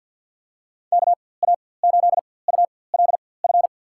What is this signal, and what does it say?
Morse code, heard as a keyed tone, about 49 wpm, spaced as for 23 wpm, 700 Hz. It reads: KA8ULF
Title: USA Call Signs - 6 Characters - 4x Character Spacing - 50wpm